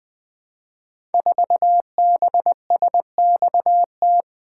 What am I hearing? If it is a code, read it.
Morse code, heard as a keyed tone, about 20 wpm, 700 Hz. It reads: 4BSXT
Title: Code Groups - 5 Letter:Numbers - 20wpm